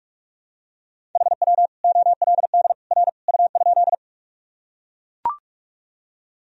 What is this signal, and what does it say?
Morse code, heard as a keyed tone, about 42 wpm, 700 Hz. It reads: HW OLD R U?
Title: Sentences from Top 200 Words and Abbreviations 45wpm